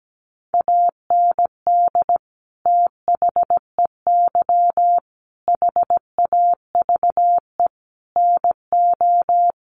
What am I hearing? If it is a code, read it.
Morse code, heard as a keyed tone, about 17 wpm, 700 Hz. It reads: AND THEY HAVE NO
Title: Sentences from Top 1400 Words 17wpm